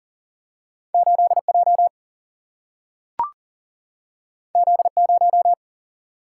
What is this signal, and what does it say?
Morse code, heard as a keyed tone, about 40 wpm, 700 Hz. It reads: 8J Z0
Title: Single-Letter Single-Number 40wpm